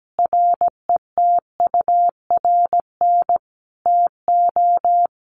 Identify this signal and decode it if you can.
Morse code, heard as a keyed tone, about 17 wpm, 700 Hz. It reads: RETURN TO
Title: Sentences from Top 800 Words 17wpm